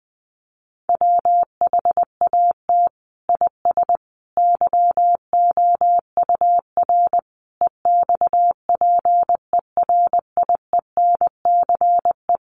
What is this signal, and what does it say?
Morse code, heard as a keyed tone, about 19 wpm, 700 Hz. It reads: WHAT IS YOUR EXPERIENCE
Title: Sentences from Top 600 Words 20wpm